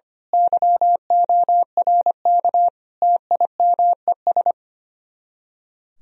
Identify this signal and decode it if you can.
Morse code, heard as a keyed tone, about 24 wpm, 700 Hz. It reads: YORK TIMES
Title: Speed-Racing - Sets of 5 Words 25wpm